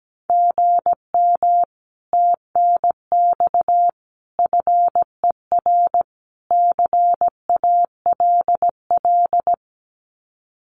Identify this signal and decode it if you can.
Morse code, heard as a keyed tone, about 17 wpm, 700 Hz. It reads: GM TNX FER CALL
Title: Common QSO Phrases Encore 17wpm